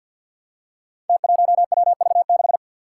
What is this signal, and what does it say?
Morse code, heard as a keyed tone, about 46 wpm, 700 Hz. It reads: T1WV6
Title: Five Challenging Letter-Number Combinations 50wpm